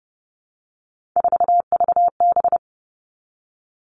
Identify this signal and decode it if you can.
Morse code, heard as a keyed tone, about 30 wpm, 700 Hz. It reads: 4VB
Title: Three Challenging Letter-Number Combinations 30wpm